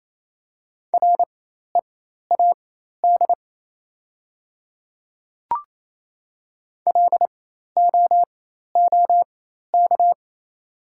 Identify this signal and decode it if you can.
Morse code, heard as a keyed tone, about 27 wpm, spaced as for 13 wpm, 700 Hz. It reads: READ LOOK
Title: 4x Character Spacing - Top 200 Words - 28wpm